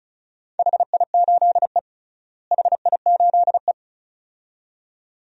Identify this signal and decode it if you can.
Morse code, heard as a keyed tone, about 34 wpm, 700 Hz. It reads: HI8E HI8E
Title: DXCC-100 Call Signs - 4 Characters - Repeated 2x - 35wpm